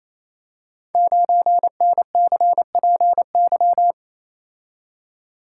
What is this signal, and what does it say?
Morse code, heard as a keyed tone, about 29 wpm, 700 Hz. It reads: 9NCPY
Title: Code Groups - 5 Letter:Numbers - 28wpm